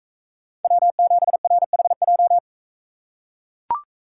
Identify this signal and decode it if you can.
Morse code, heard as a keyed tone, about 40 wpm, 700 Hz. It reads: W7RHJ